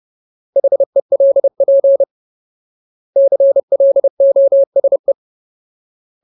Morse code, heard as a keyed tone, about 30 wpm, 550 Hz. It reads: HELP CLOSE
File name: Ultimate Word List / 30WPM Ultimate